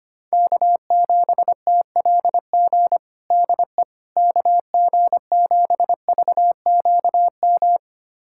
Morse code, heard as a keyed tone, about 25 wpm, 700 Hz. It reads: K7TLG DE KG7VQM
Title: USA Callsign DE Callsign 25wpm